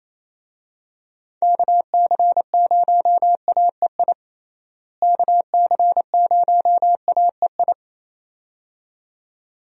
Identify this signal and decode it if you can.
Morse code, heard as a keyed tone, about 28 wpm, 700 Hz. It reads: KC0AEI KC0AEI
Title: USA Call Signs - 6 Characters - Repeated 2x - 28wpm